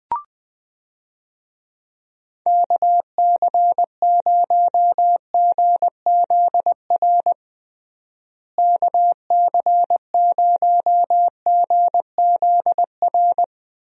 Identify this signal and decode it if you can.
Morse code, heard as a keyed tone, about 20 wpm, 700 Hz. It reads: KC0GZR KC0GZR
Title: USA Call Signs - 6 Characters - Repeated 2x - 20wpm